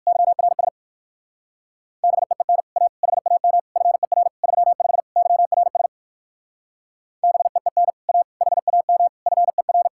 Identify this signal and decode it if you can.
Morse code, heard as a keyed tone, about 58 wpm, 700 Hz. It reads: YRS BEEN A HAM FER 35 YRS BEEN A HAM FER